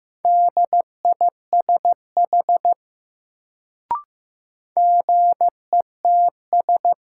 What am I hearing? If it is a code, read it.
Morse code, heard as a keyed tone, about 15 wpm, 700 Hz. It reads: DISH GETS
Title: Sentences from Top 1300 Words - New Word Drill Alt - 15wpm